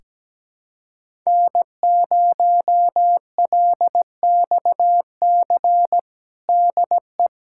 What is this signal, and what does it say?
Morse code, heard as a keyed tone, about 17 wpm, 700 Hz. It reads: N0LXC DE
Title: USA Callsign DE Callsign - 5 Characters - 17wpm